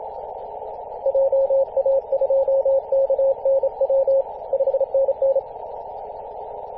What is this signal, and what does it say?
Morse code, heard as a keyed tone, about 27 wpm, 540 Hz, with radio noise. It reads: JA2KNW 5NN
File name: posts / 2016-arp-15-2115z-vk9l-g7vjr-18-081mhz-cw